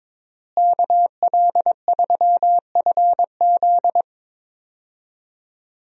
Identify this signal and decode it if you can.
Morse code, heard as a keyed tone, about 22 wpm, 700 Hz. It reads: KL3FZ
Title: USA Call Signs - 5 Characters - Repeated 3x - 22wpm